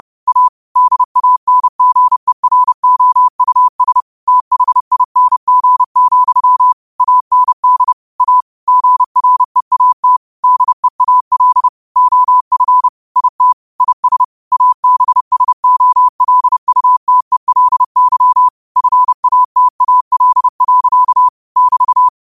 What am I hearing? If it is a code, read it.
Morse code, heard as a keyed tone, about 30 wpm, 1000 Hz. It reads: A DANGEROUS THING, AND A GREAT DEAL OF IT IS ABSOLUTELY FATAL. <BT>
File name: qotdfiles / 30wpm